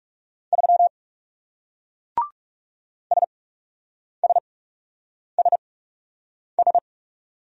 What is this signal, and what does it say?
Morse code, heard as a keyed tone, about 41 wpm, spaced as for 8 wpm, 700 Hz. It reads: 3 SSSS